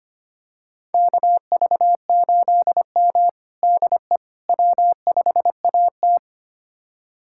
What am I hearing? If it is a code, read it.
Morse code, heard as a keyed tone, about 25 wpm, 700 Hz. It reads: KV8M DE W5AT
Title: USA Callsign DE Callsign - 4 Characters - 25wpm